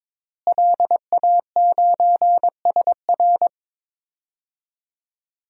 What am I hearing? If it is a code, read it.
Morse code, heard as a keyed tone, about 22 wpm, 700 Hz. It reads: LA9SR
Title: DXCC-100 Call Signs - 5 Characters - Repeated 3x - 22wpm